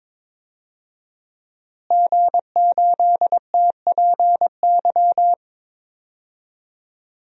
Morse code, heard as a keyed tone, about 22 wpm, 700 Hz. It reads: G8TPY
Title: DXCC-100 Call Signs - 5 Characters - Repeated 3x - 22wpm